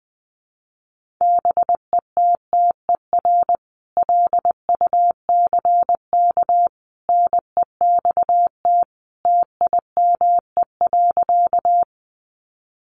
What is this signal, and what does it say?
Morse code, heard as a keyed tone, about 19 wpm, 700 Hz. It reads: BETTER LUCK NEXT TIME.
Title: Sentences from Top 1500 Words 20wpm